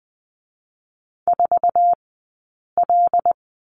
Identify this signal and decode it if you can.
Morse code, heard as a keyed tone, about 19 wpm, 700 Hz. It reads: 4 L
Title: 086 Lesson 15d - Number 5 - Two Characters 20wpm